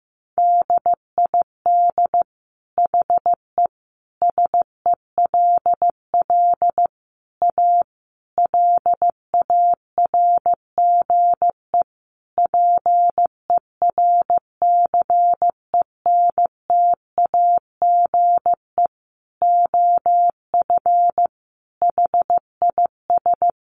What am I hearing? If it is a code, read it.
Morse code, heard as a keyed tone, about 14 wpm, 700 Hz. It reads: DID HE SELL A LARGE PERCENTAGE OF HIS
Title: Sentences from Top 1300 Words 15wpm